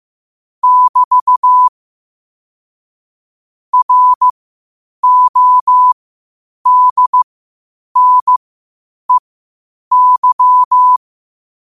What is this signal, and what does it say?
Morse code, heard as a keyed tone, about 15 wpm, spaced as for 8 wpm, 1000 Hz. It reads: <BT> RODNEY
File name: qotdfiles / 8wpm